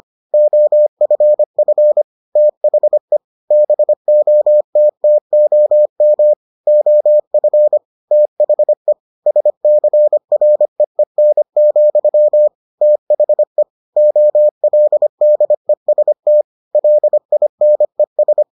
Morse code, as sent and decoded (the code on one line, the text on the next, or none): --- ..-. ..-. / - .... / -... --- - - --- -- / --- ..-. / - .... / ... -.-. .-. . -. --..-- / - .... / --- .-.. -.. ... - / .-.. .. -. ...
OFF THE BOTTOM OF THE SCREEN, THE OLDEST LINES